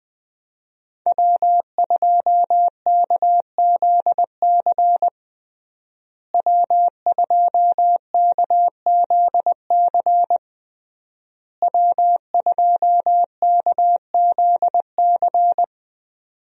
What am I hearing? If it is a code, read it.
Morse code, heard as a keyed tone, about 20 wpm, 700 Hz. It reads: W2KZC W2KZC W2KZC
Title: USA Call Signs - 5 Characters - Repeated 3x - 20wpm